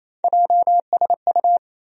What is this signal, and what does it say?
Morse code, heard as a keyed tone, about 27 wpm, 700 Hz. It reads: JSU
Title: USA Call Sign Postfixes - 3 Letters 28wpm